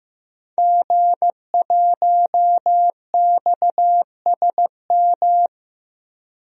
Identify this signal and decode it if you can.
Morse code, heard as a keyed tone, about 15 wpm, 700 Hz. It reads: G1XSM
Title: DXCC-100 Call Signs - 5 Characters - Repeated 2x - 15wpm